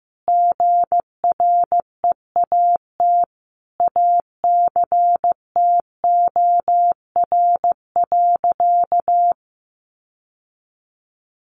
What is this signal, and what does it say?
Morse code, heard as a keyed tone, about 15 wpm, 700 Hz. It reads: GREAT ACTOR.
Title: Sentences from Top 1500 Words 15wpm